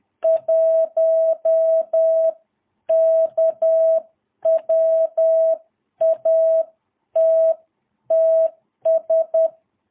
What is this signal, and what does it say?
Morse code, heard as a keyed tone, about 10 wpm, spaced as for 9 wpm, 650 Hz. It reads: 1K WATTS
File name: jh8cbh / 2022cw